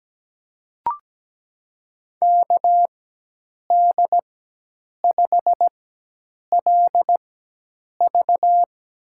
Morse code, heard as a keyed tone, about 17 wpm, spaced as for 8 wpm, 700 Hz. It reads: KD5LV
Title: USA Call Signs - 5 Characters - 4x Character Spacing - 17wpm